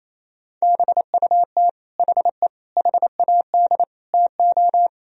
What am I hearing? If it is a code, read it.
Morse code, heard as a keyed tone, about 28 wpm, 700 Hz. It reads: BUT HE HAD TO
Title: Sentences from Top 100 Words and Abbreviations 28wpm